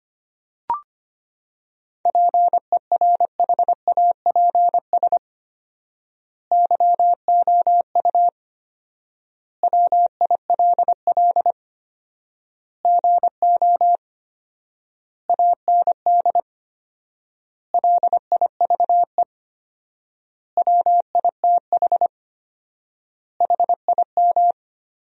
Morse code, heard as a keyed tone, about 25 wpm, 700 Hz. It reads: PERHAPS YOU WILL GO AND LIVE WITH HIM
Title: 4x Word-Spacing - Sentences from Top 500 Words - 25wpm